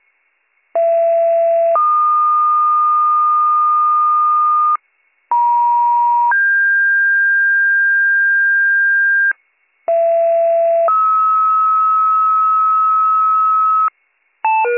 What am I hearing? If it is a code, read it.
Two-tone page, heard as a one-second tone, then a three-second tone, about 670 then 1220 Hz, about 950 then 1600 Hz, about 670 then 1290 Hz. It is heard from 0.7 to 13.9 s.